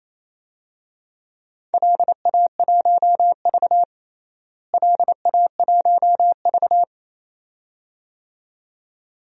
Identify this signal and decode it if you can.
Morse code, heard as a keyed tone, about 28 wpm, 700 Hz. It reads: LA1V LA1V
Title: DXCC-100 Call Signs - 4 Characters - Repeated 2x - 28wpm